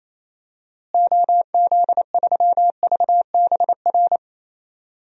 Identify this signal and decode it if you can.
Morse code, heard as a keyed tone, about 27 wpm, 700 Hz. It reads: OZ3VBR
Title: DXCC-100 Call Signs - 6 Characters - Repeated 2x - 28wpm